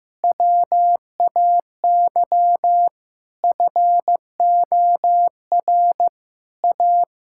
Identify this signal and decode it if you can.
Morse code, heard as a keyed tone, about 15 wpm, 700 Hz. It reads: WAY FOR A
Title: Sentences from Top 1000 Words - Review Part 6 - 15wpm